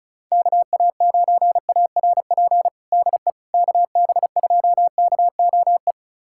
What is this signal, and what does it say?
Morse code, heard as a keyed tone, about 34 wpm, 700 Hz. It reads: KA9ARP DE KB2KOE
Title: USA Callsign DE Callsign - 6 Characters - 35wpm